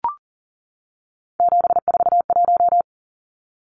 Morse code, heard as a keyed tone, about 40 wpm, 700 Hz. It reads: 741
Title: Numbers - 3-Digit 40wpm